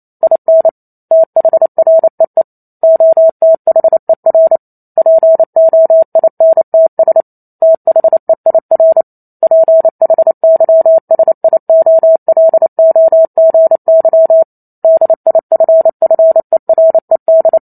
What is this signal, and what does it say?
Morse code, heard as a keyed tone, about 28 wpm, 650 Hz. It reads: IN THREE OTHER POINTS THEIR PHYSIOLOGY DIFFERED